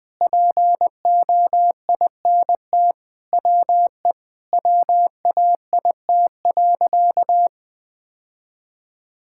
Morse code, heard as a keyed tone, about 20 wpm, 700 Hz. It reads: POINT WE WAIT.
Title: Sentences with 3-Word Phrases - 20wpm